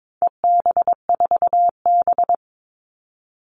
Morse code, heard as a keyed tone, about 22 wpm, 700 Hz. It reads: EB4B
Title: All Cut Numbers - 4-Digit 22wpm